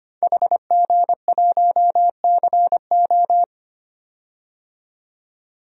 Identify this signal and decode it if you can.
Morse code, heard as a keyed tone, about 25 wpm, 700 Hz. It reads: HG1CO
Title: DXCC-100 Call Signs - 5 Characters - 25wpm